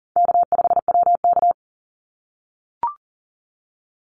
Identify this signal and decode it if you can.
Morse code, heard as a keyed tone, about 40 wpm, 700 Hz. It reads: K5WK
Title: Easy Call Signs - 40wpm